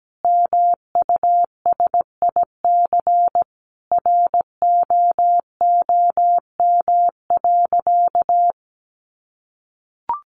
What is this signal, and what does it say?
Morse code, heard as a keyed tone, about 17 wpm, 700 Hz. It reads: MUSIC ROOM.
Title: Sentences from Top 700 Words 17wpm